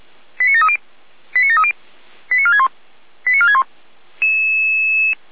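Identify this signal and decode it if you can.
Five-tone selective call (ZVEI), heard as quick runs of five tones, tones of about 70 ms. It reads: 89730 89730 89461 89461
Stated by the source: ZVEI five-tone call 89127 received by SDR, recorded by feuerwehr-orlamuende.org fire brigade server.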